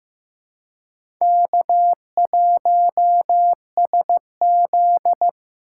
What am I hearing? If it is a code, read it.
Morse code, heard as a keyed tone, about 15 wpm, 700 Hz. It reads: K1SZ
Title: USA Call Signs - 4 Characters - 8x Character Spacing - 15wpm